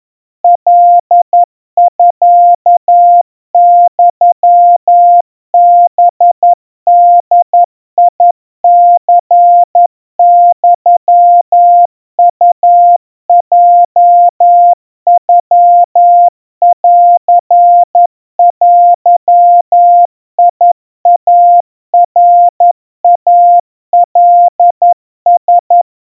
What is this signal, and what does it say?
Morse code (Japanese Wabun code), heard as a keyed tone, about 11 wpm, 700 Hz. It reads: カミユハボニユウヲノンデイナイカラ
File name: kawa / Morse sample / kamyu 40 cpm 700 Hz Crispy